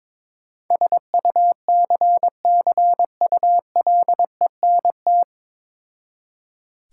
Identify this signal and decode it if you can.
Morse code, heard as a keyed tone, about 22 wpm, 690 Hz. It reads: SUCCULENT
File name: Difficult Words with Hints - 22wpm